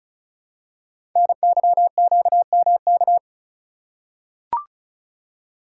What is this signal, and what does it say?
Morse code, heard as a keyed tone, about 36 wpm, 700 Hz. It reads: NYQMK
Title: Code Groups - 5 Letters - 35wpm